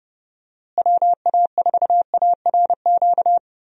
Morse code, heard as a keyed tone, about 30 wpm, 700 Hz. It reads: WA4ARQ
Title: USA Call Signs - 6 Characters - Repeated 3x - 30wpm